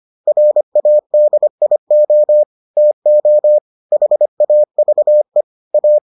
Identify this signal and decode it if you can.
Morse code, heard as a keyed tone, about 25 wpm, 600 Hz. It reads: RADIO TO HAVE A